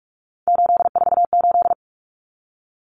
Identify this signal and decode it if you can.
Morse code, heard as a keyed tone, about 44 wpm, 700 Hz. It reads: 848
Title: Common Cut Numbers - 3-Digit 45wpm